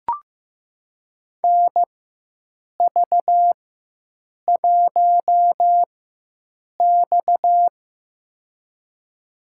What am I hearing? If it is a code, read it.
Morse code, heard as a keyed tone, about 15 wpm, spaced as for 7 wpm, 700 Hz. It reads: NV1X
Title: USA Call Signs - 4 Characters - 4x Character Spacing - 15wpm